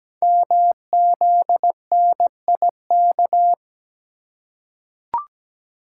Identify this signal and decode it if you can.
Morse code, heard as a keyed tone, about 17 wpm, 700 Hz. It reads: MZNIK
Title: Code Groups - 5 Letters - 17wpm